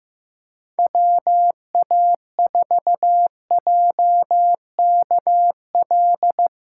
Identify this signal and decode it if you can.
Morse code, heard as a keyed tone, about 15 wpm, 700 Hz. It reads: WA4JKL
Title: K1USN SST QSO Practice - 15wpm